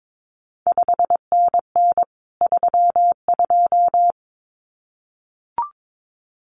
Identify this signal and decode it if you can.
Morse code, heard as a keyed tone, about 22 wpm, 700 Hz. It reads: 5NN 32
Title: CQ WW DX Contest Exchange Encore - 22wpm